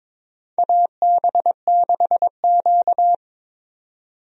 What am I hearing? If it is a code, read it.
Morse code, heard as a keyed tone, about 22 wpm, 700 Hz. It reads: AB6Q
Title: USA Call Signs - 4 Characters - 22wpm